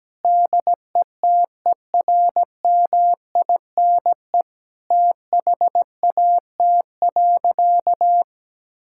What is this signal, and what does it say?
Morse code, heard as a keyed tone, about 17 wpm, 700 Hz. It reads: DETERMINE THAT.
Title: Sentences from Top 1000 Words - Review Part 5 - 17wpm